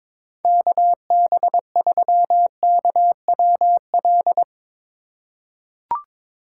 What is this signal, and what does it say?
Morse code, heard as a keyed tone, about 22 wpm, 700 Hz. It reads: KB3KWL
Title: USA Call Signs - 6 Characters - 4x Character Spacing - 22wpm